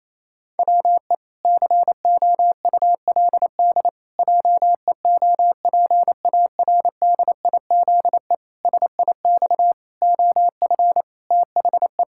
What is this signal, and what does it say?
Morse code, heard as a keyed tone, about 28 wpm, 700 Hz. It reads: WE COULD JEOPARDIZE SIX OF THE
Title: Pangram Challenge v2 - 28wpm